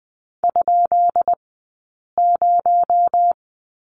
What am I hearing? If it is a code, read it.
Morse code, heard as a keyed tone, about 20 wpm, 700 Hz. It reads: ? 0
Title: 217 Lesson 35d - Number 0 - Two Characters 20wpm